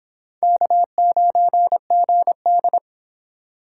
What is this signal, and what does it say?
Morse code, heard as a keyed tone, about 26 wpm, 700 Hz. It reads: K9GD